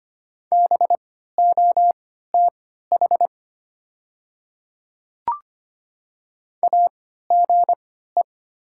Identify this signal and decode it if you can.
Morse code, heard as a keyed tone, about 25 wpm, spaced as for 14 wpm, 700 Hz. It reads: BOTH AGE
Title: 3x Character Spacing - Top 500 Words - 25wpm